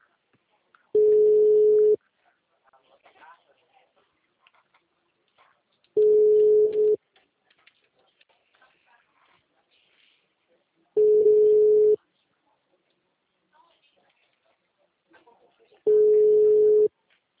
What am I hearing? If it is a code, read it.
Ringback tone (Europe, 425 Hz), with speech.